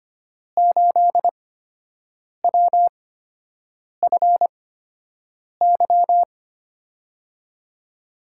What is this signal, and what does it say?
Morse code, heard as a keyed tone, about 25 wpm, spaced as for 7 wpm, 700 Hz. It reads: 8WFY
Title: USA Call Signs - 5 Characters - 8x Character Spacing - 25wpm